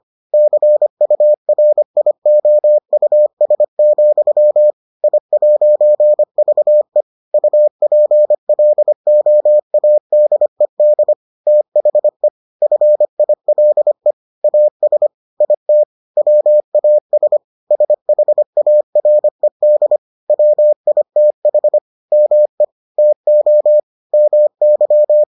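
Morse code, heard as a keyed tone, about 25 wpm, 600 Hz. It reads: CURIOUS, I'VE UPLOADED THE FILE AS IT WAS SHARED WITH ME TO MY
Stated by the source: Morse edition of the Foundations of Amateur Radio episodes.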